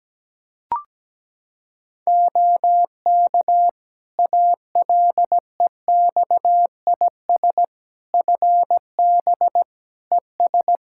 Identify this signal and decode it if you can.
Morse code, heard as a keyed tone, about 17 wpm, 700 Hz. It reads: OK ALEXIS FB ES
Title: Practice QSO - Exchange 2 - Info - 17wpm